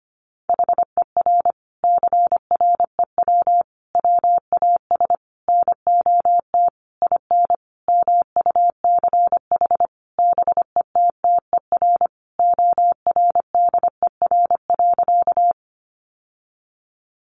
Morse code, heard as a keyed tone, about 25 wpm, 700 Hz. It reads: HER CREW WAS NOT IN MUCH BETTER ORDER.
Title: Sentences from Top 1500 Words 25wpm